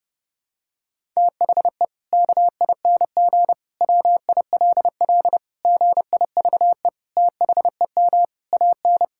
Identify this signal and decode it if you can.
Morse code, heard as a keyed tone, about 30 wpm, 700 Hz. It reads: THE KING WILL GIVE THEM AN
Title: Sentences from Top 1000 Words - Review Part 5 - 30wpm